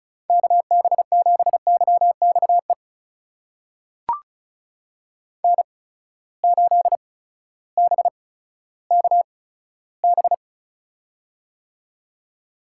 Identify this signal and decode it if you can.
Morse code, heard as a keyed tone, about 36 wpm, spaced as for 16 wpm, 700 Hz. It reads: KB7YXE N8BKB
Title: USA Call Signs - 8x Character Spacing - 35wpm